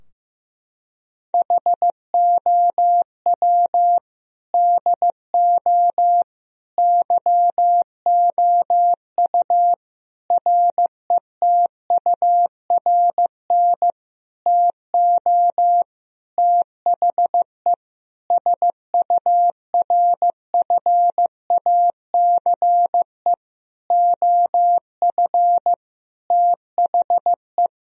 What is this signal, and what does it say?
Morse code, heard as a keyed tone, about 15 wpm, 700 Hz. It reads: HOW DO YOU RETURN TO THE SURFACE OF THE
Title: Sentences from Top 1000 Words - Review Part 3 - 15wpm